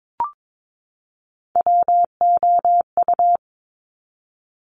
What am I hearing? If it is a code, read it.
Morse code, heard as a keyed tone, about 22 wpm, 700 Hz. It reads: WOU
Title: Top Three Letter Combinations in Words 22wpm